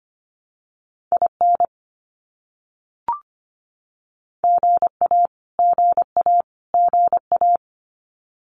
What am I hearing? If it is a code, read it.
Morse code, heard as a keyed tone, about 25 wpm, 700 Hz. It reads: IN GA GA GA
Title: US States Repeated 3x 25wpm